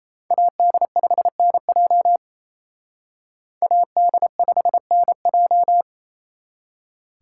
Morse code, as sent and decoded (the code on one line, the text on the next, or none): .- -.. ..... -. .--- / .- -.. ..... -. .---
AD5NJ AD5NJ